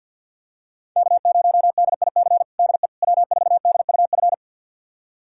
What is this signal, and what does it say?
Morse code, heard as a keyed tone, about 52 wpm, 700 Hz. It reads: K0DIC DE W4DUF
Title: USA Callsign DE Callsign - 5 Characters - 50wpm